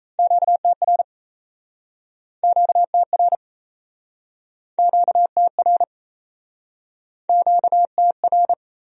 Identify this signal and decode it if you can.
Morse code, heard as a keyed tone, about 38 wpm, 700 Hz. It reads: QTR QTR QTR QTR